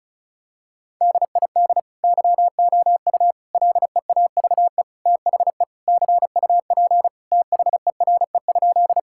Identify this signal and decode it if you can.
Morse code, heard as a keyed tone, about 36 wpm, 700 Hz. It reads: DID YOU LEAVE THE CUP THERE?
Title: Sentences from Top 1600 Words 35wpm